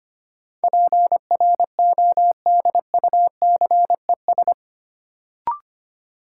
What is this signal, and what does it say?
Morse code, heard as a keyed tone, about 25 wpm, 700 Hz. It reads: PRODUCES
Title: Sentences from Top 1000 Words - Review Part 6 - New Word Drill - 25wpm